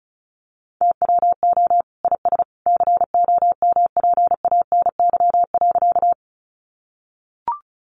Morse code, heard as a keyed tone, about 34 wpm, 700 Hz. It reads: TWO IS COMPANY.
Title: Sentences from Top 600 Words 35wpm